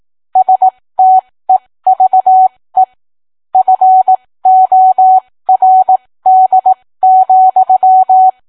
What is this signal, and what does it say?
Morse code, heard as a keyed tone, about 18 wpm, spaced as for 15 wpm, 750 Hz. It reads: STEVE FORD,